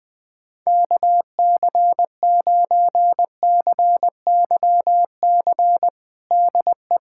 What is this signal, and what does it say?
Morse code, heard as a keyed tone, about 20 wpm, 690 Hz. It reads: KC9CYC DE
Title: USA Callsign DE Callsign 20wpm